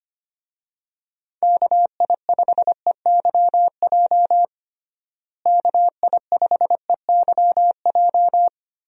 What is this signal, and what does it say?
Morse code, heard as a keyed tone, about 25 wpm, 700 Hz. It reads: KI5EYJ KI5EYJ
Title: USA Call Signs - 6 Characters - Repeated 2x - 25wpm